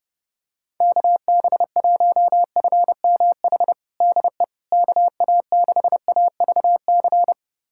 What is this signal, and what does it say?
Morse code, heard as a keyed tone, about 30 wpm, 700 Hz. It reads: KB1FMH DE KA6AVC
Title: USA Callsign DE Callsign - 6 Characters - 30wpm